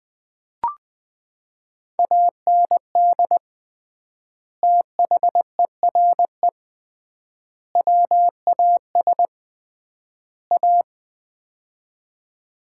Morse code, heard as a keyed tone, about 21 wpm, 700 Hz. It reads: AND THERE WAS A
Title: 3x Word-Spacing - Sets of 4 Words - 20wpm